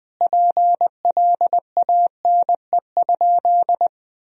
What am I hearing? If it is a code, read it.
Morse code, heard as a keyed tone, about 20 wpm, 700 Hz. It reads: PLANE?
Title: Sentences from Top 1000 Words - Review Part 3 - 20wpm